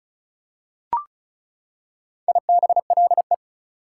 Morse code, heard as a keyed tone, about 36 wpm, 700 Hz. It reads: IBLE
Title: Prefixes and Suffixes 35wpm